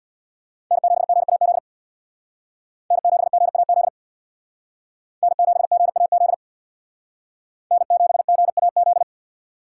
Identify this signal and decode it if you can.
Morse code, heard as a keyed tone, about 65 wpm, 700 Hz. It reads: N7GAZ N7GAZ N7GAZ N7GAZ